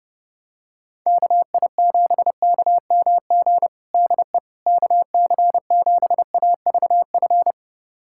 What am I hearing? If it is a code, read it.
Morse code, heard as a keyed tone, about 30 wpm, 700 Hz. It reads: KI7KMG DE KC7AVF
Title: USA Callsign DE Callsign - 6 Characters - 30wpm